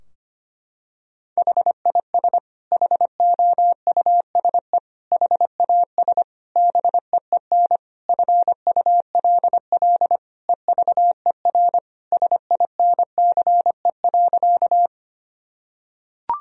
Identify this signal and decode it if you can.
Morse code, heard as a keyed tone, about 25 wpm, 700 Hz. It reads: HIS HOUSE HAS BEEN FULL EVER SINCE.
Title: Sentences from Top 1000 Words - Review Part 3 - 25wpm